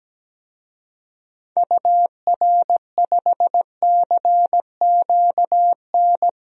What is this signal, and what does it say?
Morse code, heard as a keyed tone, about 17 wpm, 700 Hz. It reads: UR5CQN
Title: DXCC-100 Call Signs - 6 Characters - Repeated 3x - 17wpm